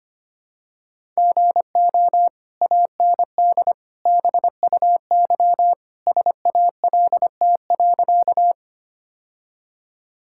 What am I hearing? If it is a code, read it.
Morse code, heard as a keyed tone, about 25 wpm, 700 Hz. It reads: GO AND BUY SALT.
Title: Sentences from Top 1000 Words - Review Part 5 - 25wpm